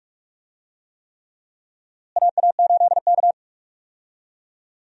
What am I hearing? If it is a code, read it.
Morse code, heard as a keyed tone, about 46 wpm, 700 Hz. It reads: AA8K
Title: USA Call Signs - 4 Characters - Repeated 3x - 45wpm